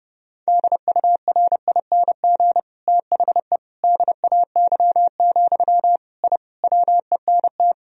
Morse code, heard as a keyed tone, about 30 wpm, 700 Hz. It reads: DURING THE DAY, I WENT
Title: Sentences from Top 1000 Words - Review Part 2 - 30wpm